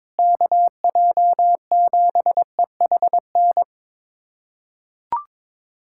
Morse code, heard as a keyed tone, about 22 wpm, 700 Hz. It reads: KJ7EHN
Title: USA Callsign DE Callsign 22wpm